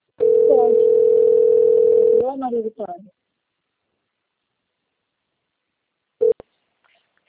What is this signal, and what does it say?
Ringback tone (North America), 440 + 480 Hz, with speech.
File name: rg-1600-unknown- - - .39270